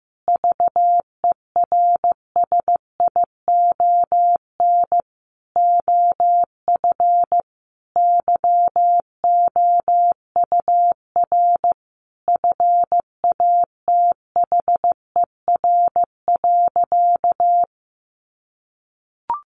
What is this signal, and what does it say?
Morse code, heard as a keyed tone, about 15 wpm, 700 Hz. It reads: VERSION OF YOUR FATHER.
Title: Sentences from Top 1400 Words 15wpm